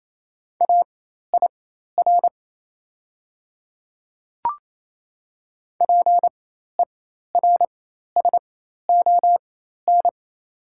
Morse code, heard as a keyed tone, about 27 wpm, spaced as for 13 wpm, 700 Hz. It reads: AIR PERSON
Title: 4x Character Spacing - Top 500 Words - 28wpm